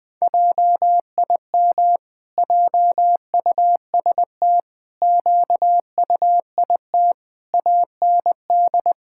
Morse code, heard as a keyed tone, about 20 wpm, 700 Hz. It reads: JIM JUST QUIT AND
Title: Pangram Challenge v2 - 20wpm